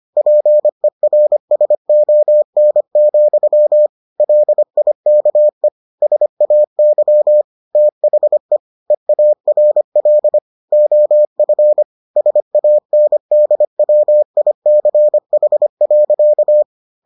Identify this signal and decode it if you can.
Morse code, heard as a keyed tone, about 25 wpm, 600 Hz. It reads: PERSON, LIKE SAY THE EARL OF SANDWICH.